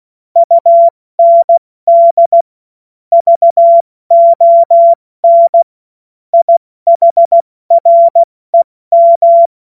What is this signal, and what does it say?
Morse code, heard as a keyed tone, about 16 wpm, spaced as for 14 wpm, 680 Hz. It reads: UND VON IHREM
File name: Morsekurs / Witze / Witz 12 März 2024 70 F80